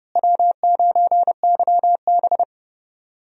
Morse code, heard as a keyed tone, about 30 wpm, 700 Hz. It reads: W9YB
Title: USA Call Signs - 4 Characters - Repeated 2x - 30wpm